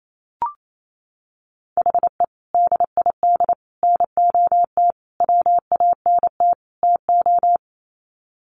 Morse code, heard as a keyed tone, about 28 wpm, 700 Hz. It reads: HE DID NOT WANT TO
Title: Sets of 5 Words 28wpm